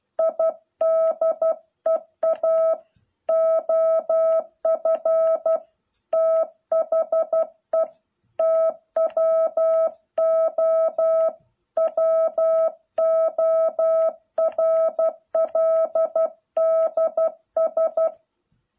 Morse code, heard as a keyed tone, about 12 wpm, 660 Hz. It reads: IDEA OF THE TWO WORLDS